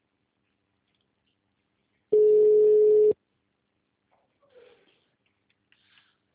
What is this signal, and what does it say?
Ringback tone (Europe, 425 Hz).